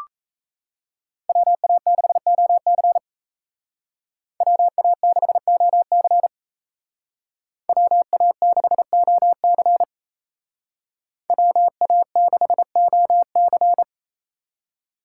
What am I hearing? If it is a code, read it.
Morse code, heard as a keyed tone, about 36 wpm, 700 Hz. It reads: WA6OC WA6OC WA6OC WA6OC